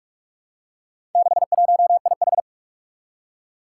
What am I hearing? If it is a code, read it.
Morse code, heard as a keyed tone, about 41 wpm, 700 Hz. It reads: 61IH
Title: Four Challenging Letter-Number Combinations 45wpm